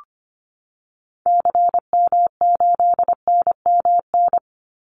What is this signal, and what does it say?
Morse code, heard as a keyed tone, about 25 wpm, 700 Hz. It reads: CM8NMN
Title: Call Signs Repeated 2x - 25wpm